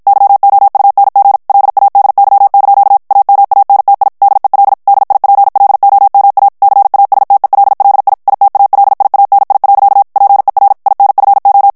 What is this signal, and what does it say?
Morse code, heard as a keyed tone, about 54 wpm, 800 Hz. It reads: YOUNG LADY) AMANTI DEL DIPLOMA CASTELLI ITALIANI. PER INFO
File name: ITALIEN texte clair 1 vitesse 50 mots minute